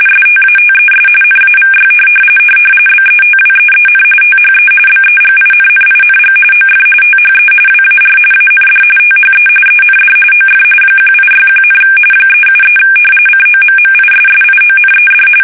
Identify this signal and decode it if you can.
RTTY, about 45.45 baud. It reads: STATES NAVY BETWEEN THE BEGINNING OF THE WAR AND THE END OF 1861, 373 COMMISSIONED
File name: fsk850 / usnavy3